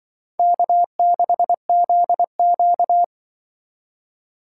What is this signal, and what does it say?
Morse code, heard as a keyed tone, about 24 wpm, 700 Hz. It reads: K6ZQ
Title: Speed Racing - USA 1x2 Call Signs 20wpm